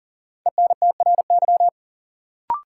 Morse code, heard as a keyed tone, about 40 wpm, 700 Hz. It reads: ENTRY
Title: Sentences from Top 1500 Words - New Word Drill - 40wpm